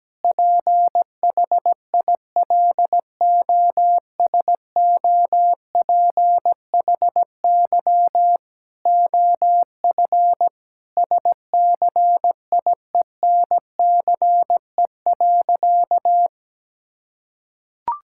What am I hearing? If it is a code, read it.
Morse code, heard as a keyed tone, about 17 wpm, 700 Hz. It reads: PHILOSOPHY OF SCIENCE.
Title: Sentences from Top 1000 Words - Review Part 6 - 17wpm